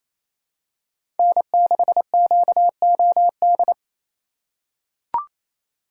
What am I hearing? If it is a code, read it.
Morse code, heard as a keyed tone, about 28 wpm, 700 Hz. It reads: N6QOD
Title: USA Call Signs - 2x Character Spacing - 28wpm